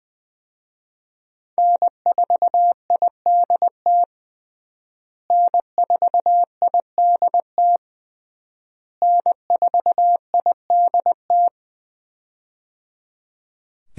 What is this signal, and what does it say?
Morse code, heard as a keyed tone, about 20 wpm, 700 Hz. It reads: N4IDT N4IDT N4IDT
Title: USA Call Signs - 5 Characters - Repeated 3x - 20wpm